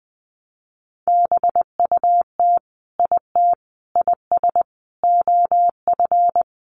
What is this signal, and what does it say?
Morse code, heard as a keyed tone, about 20 wpm, 700 Hz. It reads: BUT IT IS OF
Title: Sentences from Top 1200 Words 20wpm